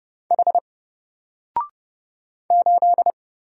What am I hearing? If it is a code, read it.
Morse code, heard as a keyed tone, about 29 wpm, 700 Hz. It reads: H 8
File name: Single-Letter-Number-30wpm